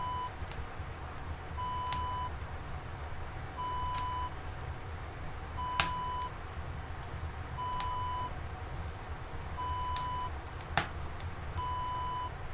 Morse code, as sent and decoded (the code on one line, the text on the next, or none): none